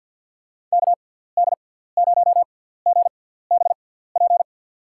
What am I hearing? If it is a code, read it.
Morse code, heard as a keyed tone, about 47 wpm, spaced as for 17 wpm, 700 Hz. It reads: KD0GBP
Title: USA Call Signs - 6 Characters - 6x Character Spacing - 50wpm